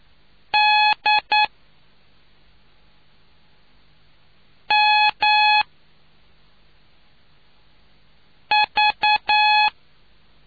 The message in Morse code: D M V

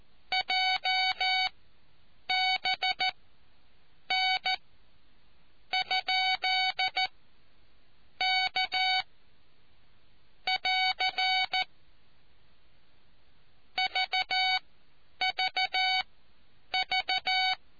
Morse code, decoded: JBN ? K <AR> VVV

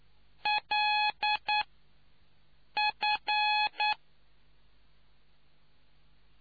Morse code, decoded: L F

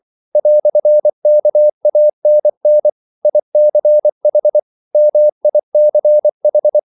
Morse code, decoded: "KANN ICH MICH

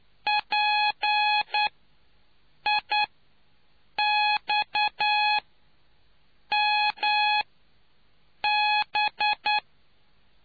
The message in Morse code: PIXMB

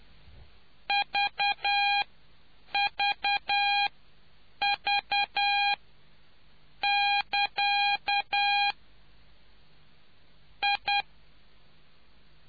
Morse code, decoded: VVV<KA> I

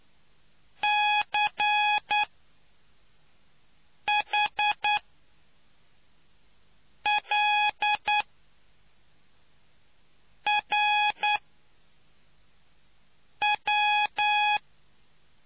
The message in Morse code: CHLRW

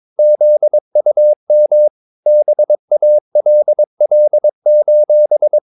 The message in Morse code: ZUM BALL:

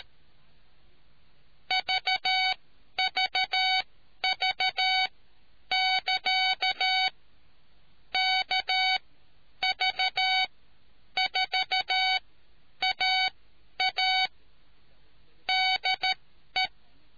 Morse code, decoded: VVV<KA> KV4AA DE

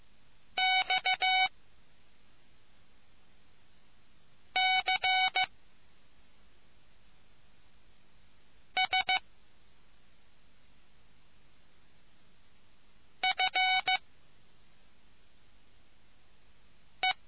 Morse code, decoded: X C S F E

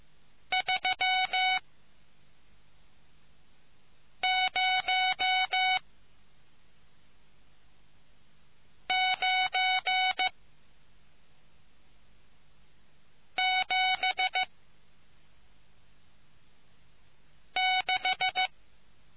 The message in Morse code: 30976

